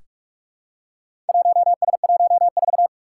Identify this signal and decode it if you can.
Morse code, heard as a keyed tone, about 44 wpm, 700 Hz. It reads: 1S14
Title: Four Challenging Letter-Number Combinations 45wpm